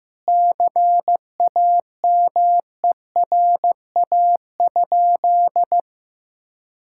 Morse code, heard as a keyed tone, about 16 wpm, 700 Hz. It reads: CAMERA?
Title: Sentences from Top 1000 Words - Review Part 5 - 15wpm